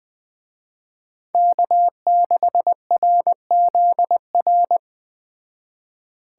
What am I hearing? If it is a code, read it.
Morse code, heard as a keyed tone, about 20 wpm, 700 Hz. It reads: K6RZR
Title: Speed Racing - USA 1x3 Call Signs 17wpm